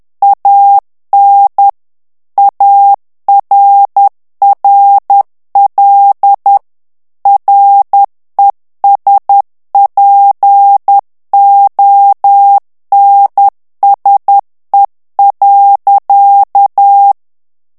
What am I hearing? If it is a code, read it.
Morse code, heard as a keyed tone, about 10 wpm, 800 Hz. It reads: AN ARRL RESPONSE.
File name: ANGLAIS texte clair 3 vitesse 10 mots minute